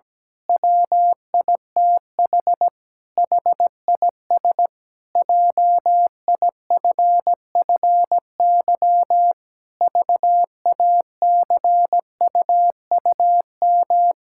Morse code, heard as a keyed tone, about 17 wpm, 700 Hz. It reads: WITH HIS JIFFY VACUUM
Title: Pangram Challenge v2 - 17wpm